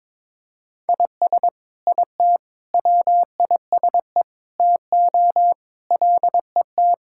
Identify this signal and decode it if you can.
Morse code, heard as a keyed tone, about 22 wpm, 700 Hz. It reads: IS IT WISE TO LET
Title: Sentences from Top 1600 Words 22wpm